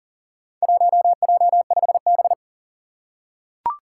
Morse code, heard as a keyed tone, about 37 wpm, 700 Hz. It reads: 1J5B
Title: Four Challenging Letter-Number Combinations 40wpm